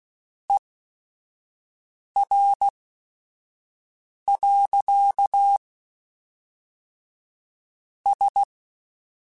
Morse code, heard as a keyed tone, about 16 wpm, spaced as for 5 wpm, 800 Hz. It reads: ER. S